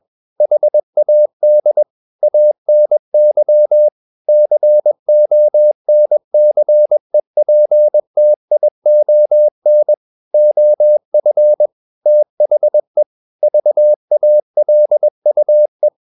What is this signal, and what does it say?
Morse code, heard as a keyed tone, about 21 wpm, 600 Hz. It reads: HAD ANY CONCEPTION OF THE VALUE
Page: CW med Gnister